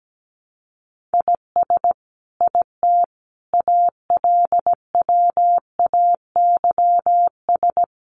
Morse code, heard as a keyed tone, about 17 wpm, 700 Hz. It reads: IS IT ALWAYS
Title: Sentences from Top 1500 Words 17wpm